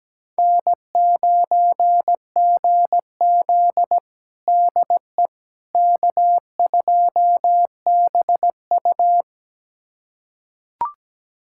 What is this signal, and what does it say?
Morse code, heard as a keyed tone, about 17 wpm, 700 Hz. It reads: N9GZ DE K2BU
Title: USA Callsign DE Callsign - 4 Characters - 17wpm